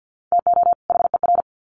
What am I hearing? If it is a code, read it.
Morse code, heard as a keyed tone, about 50 wpm, 700 Hz. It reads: TO HER
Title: Sets of 2 Words 50wpm